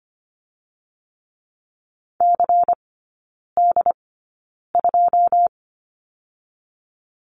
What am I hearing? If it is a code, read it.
Morse code, heard as a keyed tone, about 25 wpm, 700 Hz. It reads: C D 2
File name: Lesson 17e - Letter u - Three Characters 25wpm